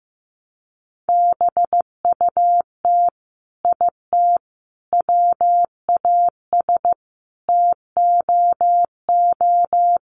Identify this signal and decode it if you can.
Morse code, heard as a keyed tone, about 15 wpm, 700 Hz. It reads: BUT IT WAS TOO
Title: Sets of 5 Words Encore2 15wpm